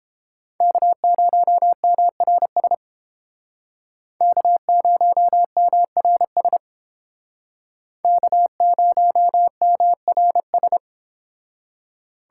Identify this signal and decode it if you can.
Morse code, heard as a keyed tone, about 29 wpm, 700 Hz. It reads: K0MRS K0MRS K0MRS